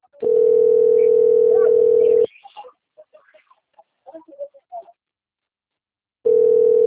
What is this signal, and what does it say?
Ringback tone (North America), 440 + 480 Hz.
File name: rg-1600-unknown- - - .24470